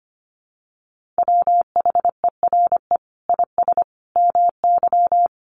Morse code, heard as a keyed tone, about 25 wpm, 700 Hz. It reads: WHERE IS MY
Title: Sentences from Top 1400 Words 25wpm